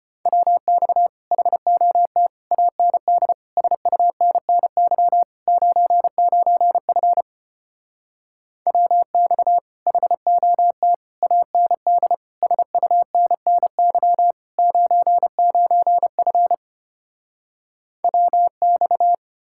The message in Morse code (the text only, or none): WX HOT AND SUNNY 99F WX HOT AND SUNNY 99F WX